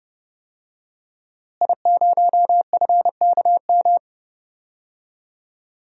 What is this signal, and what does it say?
Morse code, heard as a keyed tone, about 30 wpm, 700 Hz. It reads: I0FKM
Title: DXCC-100 Call Signs - 5 Characters - Repeated 3x - 30wpm